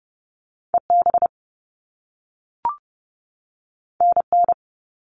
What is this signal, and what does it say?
Morse code, heard as a keyed tone, about 30 wpm, 700 Hz. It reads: EB NN
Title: All Cut Numbers - 2-Digit 30wpm